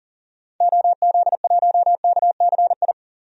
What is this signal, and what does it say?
Morse code, heard as a keyed tone, about 40 wpm, 700 Hz. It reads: OZ1KCI
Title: DXCC-100 Call Signs - 6 Characters - 40wpm